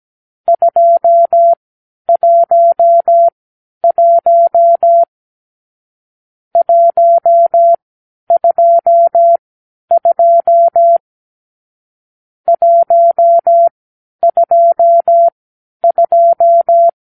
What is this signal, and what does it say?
Morse code, heard as a keyed tone, about 17 wpm, spaced as for 10 wpm, 680 Hz. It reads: 211 122 122
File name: k7qo / Lesson 44 (1-3)